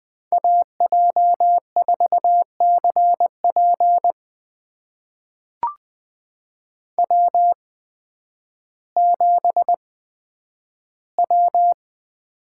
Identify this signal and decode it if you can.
Morse code, heard as a keyed tone, about 20 wpm, spaced as for 11 wpm, 700 Hz. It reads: AJ4CP W7W
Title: USA Call Signs - 5 Characters - 8x Character Spacing - 20wpm